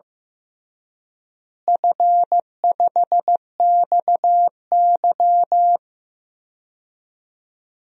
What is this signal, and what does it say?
Morse code, heard as a keyed tone, about 15 wpm, 700 Hz. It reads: F5XY
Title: DXCC-100 Call Signs - 4 Characters - Repeated 3x - 15wpm